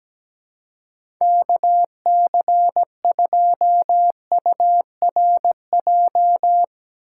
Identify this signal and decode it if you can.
Morse code, heard as a keyed tone, about 17 wpm, 700 Hz. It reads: KC2URJ
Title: USA Call Signs - 6 Characters - 3x Character Spacing - 17wpm